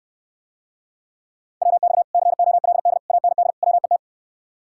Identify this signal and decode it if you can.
Morse code, heard as a keyed tone, about 69 wpm, 700 Hz. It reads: WX COLD AND WET